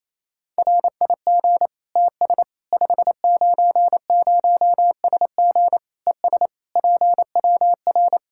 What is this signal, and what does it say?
Morse code, heard as a keyed tone, about 27 wpm, 700 Hz. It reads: RIG TS 590SG ES PWR
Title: Practice QSO - Exchange 2 - Info - 28wpm